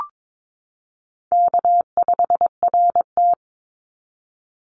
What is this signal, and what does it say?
Morse code, heard as a keyed tone, about 22 wpm, 700 Hz. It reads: K5RT
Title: Easy Call Signs Repeated 3x - 22wpm